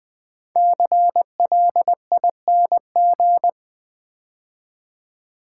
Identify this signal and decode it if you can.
Morse code, heard as a keyed tone, about 20 wpm, 700 Hz. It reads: CLING
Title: 5 Letter Words - First 3 Letters Given - 20wpm